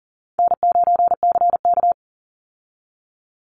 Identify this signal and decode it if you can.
Morse code, heard as a keyed tone, about 40 wpm, 700 Hz. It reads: N9CK
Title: Easy Call Signs Repeated 2x - 40wpm